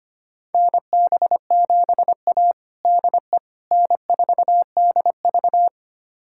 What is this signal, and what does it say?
Morse code, heard as a keyed tone, about 25 wpm, 700 Hz. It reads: NB7A DE N4DV
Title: USA Callsign DE Callsign - 4 Characters - 25wpm